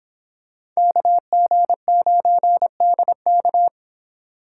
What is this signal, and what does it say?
Morse code, heard as a keyed tone, about 26 wpm, 700 Hz. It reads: KG9DK